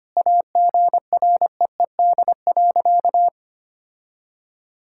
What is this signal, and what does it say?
Morse code, heard as a keyed tone, about 25 wpm, 700 Hz. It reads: AGREED.
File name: practice / 25wpm / Sentences with 3-Word Phrases - 25wpm